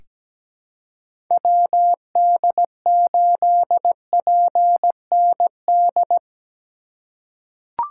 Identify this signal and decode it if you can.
Morse code, heard as a keyed tone, about 17 wpm, 700 Hz. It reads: WD8PND
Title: USA Call Signs - 6 Characters - 2x Character Spacing - 17wpm